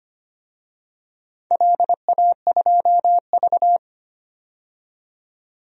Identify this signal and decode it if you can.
Morse code, heard as a keyed tone, about 25 wpm, 700 Hz. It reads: LA2V
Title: DXCC-100 Call Signs - 4 Characters - 25wpm